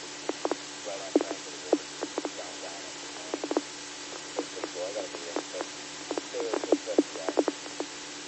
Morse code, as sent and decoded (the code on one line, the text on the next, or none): none